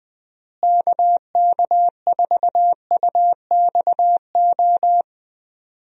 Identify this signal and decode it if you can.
Morse code, heard as a keyed tone, about 20 wpm, 700 Hz. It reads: KK4UXO
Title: USA Call Signs - 6 Characters - Repeated 3x - 20wpm